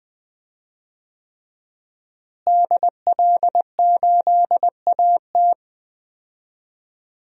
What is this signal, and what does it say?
Morse code, heard as a keyed tone, about 20 wpm, 700 Hz. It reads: DL8AT
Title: DXCC-100 Call Signs - 5 Characters - Repeated 3x - 20wpm